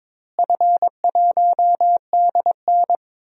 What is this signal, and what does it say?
Morse code, heard as a keyed tone, about 22 wpm, 700 Hz. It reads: F1DN
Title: DXCC-100 Call Signs - 4 Characters - Repeated 3x - 22wpm